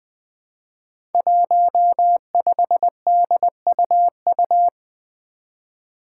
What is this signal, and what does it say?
Morse code, heard as a keyed tone, about 20 wpm, 700 Hz. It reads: 15DUU
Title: Five Challenging Letter-Number Combinations 20wpm